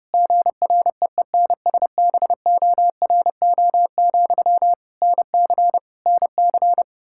Morse code, heard as a keyed tone, about 30 wpm, 700 Hz. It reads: GREENSBORO, NC NC
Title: Practice QSO - Exchange 1 - Beginning - 30wpm